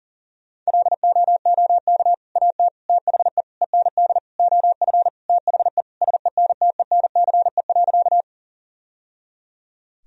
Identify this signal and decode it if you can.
Morse code, heard as a keyed tone, about 40 wpm, 700 Hz. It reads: LOOK AT THE END OF THE SENTENCE.
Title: Sentences from Top 1000 Words - Review Part 1 - 40wpm